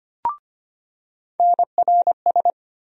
Morse code, heard as a keyed tone, about 25 wpm, 700 Hz. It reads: NRS
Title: USA Call Sign Postfixes - 3 Letters 25wpm